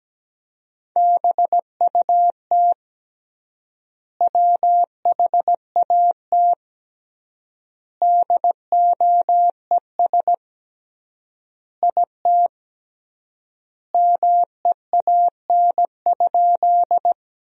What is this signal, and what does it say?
Morse code, heard as a keyed tone, about 17 wpm, 700 Hz. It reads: BUT WHAT DOES IT MEAN?
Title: 3x Word-Spacing - Sentences from Top 200 Words - 17wpm